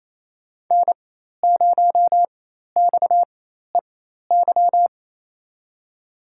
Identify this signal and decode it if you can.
Morse code, heard as a keyed tone, about 28 wpm, spaced as for 13 wpm, 700 Hz. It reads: N0XEY
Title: USA Call Signs - 4x Character Spacing - 28wpm